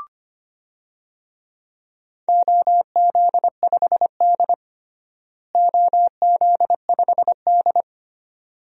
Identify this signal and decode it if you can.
Morse code, heard as a keyed tone, about 25 wpm, 700 Hz. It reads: OZ5D OZ5D
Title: DXCC-100 Call Signs - 4 Characters - Repeated 2x - 25wpm